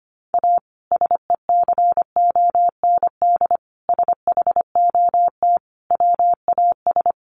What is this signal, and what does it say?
Morse code, heard as a keyed tone, about 25 wpm, 700 Hz. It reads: A SECOND SHOT WAS
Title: Sentences from Top 1100 Words 25wpm